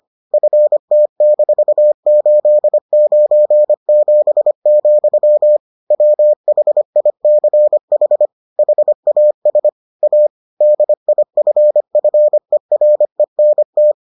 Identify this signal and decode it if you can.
Morse code, heard as a keyed tone, about 25 wpm, 600 Hz. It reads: FT-897, WHICH HAS A DIFFERENT